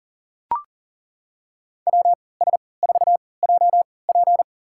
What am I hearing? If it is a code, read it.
Morse code, heard as a keyed tone, about 40 wpm, spaced as for 23 wpm, 700 Hz. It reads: WS4JP
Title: USA Call Signs - 5 Characters - 3x Character Spacing - 40wpm